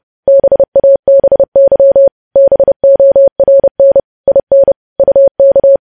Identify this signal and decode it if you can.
Morse code, heard as a keyed tone, about 30 wpm, 550 Hz. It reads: BABY BORN IN UK
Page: news headlines in morse code at 30 wpm